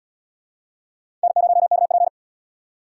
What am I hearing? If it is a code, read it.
Morse code, heard as a keyed tone, about 64 wpm, 700 Hz. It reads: N0KP